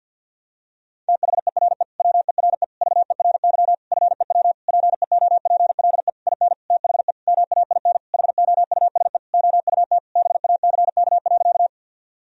Morse code, heard as a keyed tone, about 52 wpm, 700 Hz. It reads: THERE WERE VERY FEW PEOPLE IN THE MAIN HOUSE OUT BACK.
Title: Sentences from Top 1000 Words - Review Part 2 - 50wpm